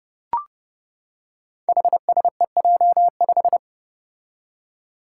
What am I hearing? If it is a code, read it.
Morse code, heard as a keyed tone, about 30 wpm, 700 Hz. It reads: HSEJ5
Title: Five Challenging Letter-Number Combinations 30wpm